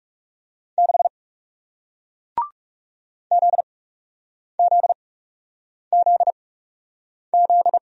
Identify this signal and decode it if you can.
Morse code, heard as a keyed tone, about 41 wpm, spaced as for 8 wpm, 700 Hz. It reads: 6 ZZZZ